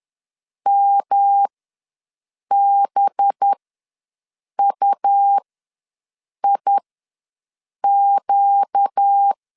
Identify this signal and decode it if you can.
Morse code, heard as a keyed tone, about 10 wpm, spaced as for 6 wpm, 800 Hz. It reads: MBUIQ
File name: LEZIONI / 5 LBVC / 5 LB ws 10wpm 3616